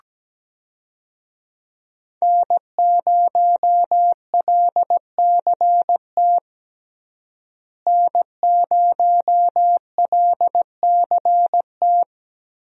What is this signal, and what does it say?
Morse code, heard as a keyed tone, about 17 wpm, 700 Hz. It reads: N0LCT N0LCT
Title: USA Call Signs - 5 Characters - Repeated 2x - 17wpm